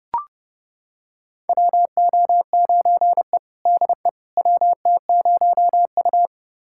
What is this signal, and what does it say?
Morse code, heard as a keyed tone, about 31 wpm, 700 Hz. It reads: WO9E DE WT0U
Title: USA Callsign DE Callsign - 4 Characters - 30wpm